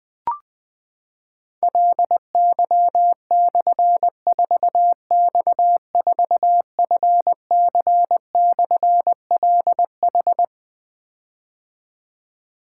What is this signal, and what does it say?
Morse code, heard as a keyed tone, about 20 wpm, 700 Hz. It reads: LY/4X4FC/LH
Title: Difficult Call Signs 20wpm